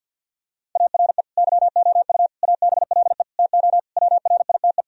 Morse code, heard as a keyed tone, about 50 wpm, 700 Hz. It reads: ARE YOU ABLE TO WRITE
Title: Sentences from Top 1000 Words - Review Part 3 - 50wpm